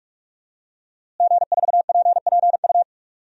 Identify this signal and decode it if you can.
Morse code, heard as a keyed tone, about 46 wpm, 700 Hz. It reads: G4PPU
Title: DXCC-100 Call Signs - 5 Characters - Repeated 2x - 45wpm